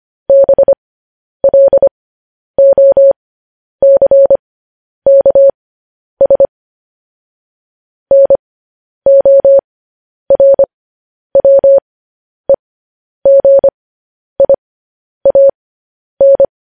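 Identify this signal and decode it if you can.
Morse code, heard as a keyed tone, about 25 wpm, spaced as for 10 wpm, 550 Hz. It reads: BLOCKS NORWEGIAN
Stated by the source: Morse code news headlines practice, at 10 WPM.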